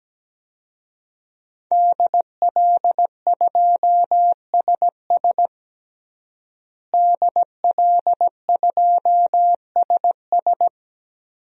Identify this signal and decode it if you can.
Morse code, heard as a keyed tone, about 17 wpm, 700 Hz. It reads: DL2SS DL2SS
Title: DXCC-100 Call Signs - 5 Characters - Repeated 3x - 17wpm